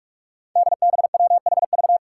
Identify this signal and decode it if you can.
Morse code, heard as a keyed tone, about 46 wpm, 700 Hz. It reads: DBWHV
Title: Five Challenging Letter-Number Combinations 45wpm